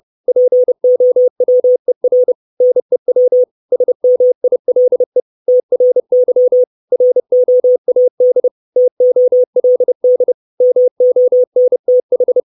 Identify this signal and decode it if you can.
Morse code, heard as a keyed tone, about 30 wpm, 500 Hz. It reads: POWER NEW SMILE TRY ROAD TOLD MONTH
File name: 632 Words 30WPM